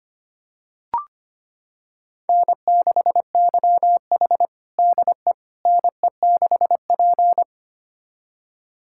Morse code, heard as a keyed tone, about 25 wpm, 700 Hz. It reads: N6YH DE NE6P
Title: USA Callsign DE Callsign - 4 Characters - 25wpm